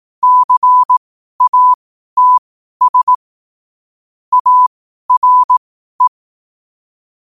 Morse code, heard as a keyed tone, about 18 wpm, spaced as for 12 wpm, 1000 Hz. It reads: CATS ARE
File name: qotdfiles / 12wpm